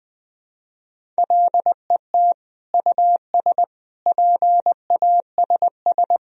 Morse code, heard as a keyed tone, about 20 wpm, 700 Hz. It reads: LET US PASS
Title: Sentences from Top 1000 Words - Review Part 1 - 20wpm